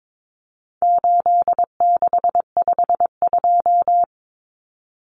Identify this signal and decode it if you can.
Morse code, heard as a keyed tone, about 22 wpm, 700 Hz. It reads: 8652
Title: Common Cut Numbers - 4-Digit 22wpm